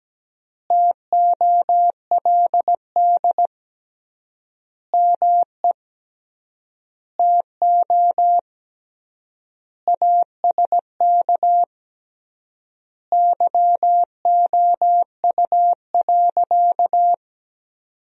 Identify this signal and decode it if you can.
Morse code, heard as a keyed tone, about 17 wpm, 700 Hz. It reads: TOLD ME TO ASK YOU.
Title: 3x Word-Spacing - Sentences from Top 400 Words - 17wpm